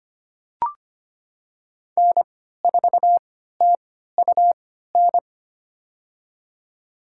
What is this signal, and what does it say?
Morse code, heard as a keyed tone, about 25 wpm, spaced as for 14 wpm, 700 Hz. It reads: N4TUN
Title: USA Call Signs - 5 Characters - 3x Character Spacing - 25wpm